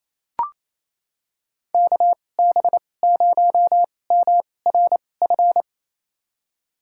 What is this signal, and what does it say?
Morse code, heard as a keyed tone, about 28 wpm, spaced as for 20 wpm, 700 Hz. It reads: KB0MRF
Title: USA Call Signs - 6 Characters - 2x Character Spacing - 28wpm